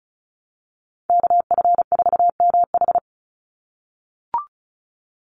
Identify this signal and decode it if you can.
Morse code, heard as a keyed tone, about 36 wpm, 700 Hz. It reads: KF4MH
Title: NA SOTA Chaser Callsigns 35wpm